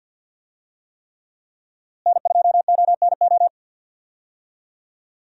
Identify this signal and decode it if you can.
Morse code, heard as a keyed tone, about 51 wpm, 700 Hz. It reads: N2ONO
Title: USA Call Signs - 5 Characters - Repeated 3x - 50wpm